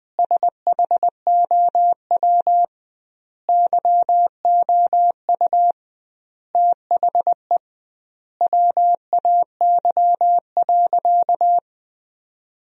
Morse code, heard as a keyed tone, about 20 wpm, 700 Hz. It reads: SHOW YOU THE WAY.
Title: 2x Word-Spacing - Sentences from Top 300 Words - 20wpm